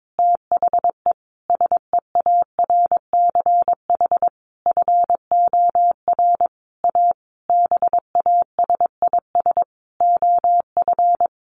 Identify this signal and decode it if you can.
Morse code, heard as a keyed tone, about 22 wpm, 700 Hz. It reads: THE SEARCH FOR A BASIS OF
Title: Sentences from Top 1500 Words 22wpm